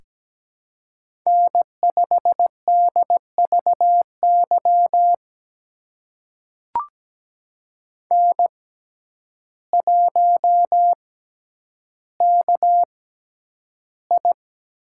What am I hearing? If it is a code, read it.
Morse code, heard as a keyed tone, about 17 wpm, spaced as for 9 wpm, 700 Hz. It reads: N5DVY N1KI
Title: USA Call Signs - 5 Characters - 6x Character Spacing - 17wpm